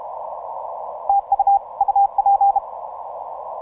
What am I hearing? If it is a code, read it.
Morse code, heard as a keyed tone, about 32 wpm, 800 Hz. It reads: TU UP
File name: simple-file-list / H44WA 10MCW